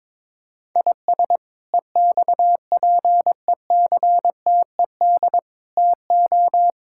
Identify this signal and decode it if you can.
Morse code, heard as a keyed tone, about 22 wpm, 700 Hz. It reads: IS EXPECTED TO
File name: practice / 22wpm / Sets of 7 Words - 22wpm